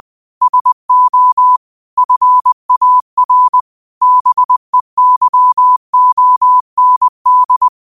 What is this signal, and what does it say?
Morse code, heard as a keyed tone, about 20 wpm, 1000 Hz. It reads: SO FAR BEYOND
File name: qotdfiles / 20wpm